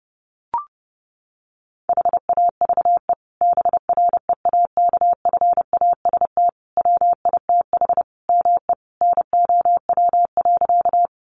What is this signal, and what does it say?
Morse code, heard as a keyed tone, about 30 wpm, 700 Hz. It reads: HAVE BREAKFAST WITH ME NOW.
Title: Sentences from Top 1500 Words 30wpm